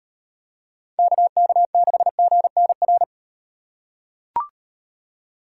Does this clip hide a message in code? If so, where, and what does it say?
Yes: Morse code from 1.0 to 3.0 s, heard as a keyed tone, about 35 wpm, 700 Hz. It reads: KK6GNR